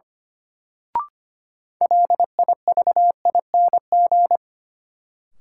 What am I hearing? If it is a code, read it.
Morse code, heard as a keyed tone, about 25 wpm, 700 Hz. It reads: LIVING
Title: Sentences from Top 1000 Words - Review Part 6 - New Word Drill - 25wpm